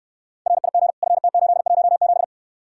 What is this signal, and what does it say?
Morse code, heard as a keyed tone, about 69 wpm, 700 Hz. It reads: RIG FT817